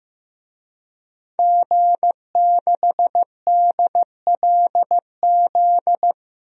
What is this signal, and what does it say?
Morse code, heard as a keyed tone, about 15 wpm, 700 Hz. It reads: G6DLZ
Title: DXCC-100 Call Signs - 5 Characters - Repeated 3x - 15wpm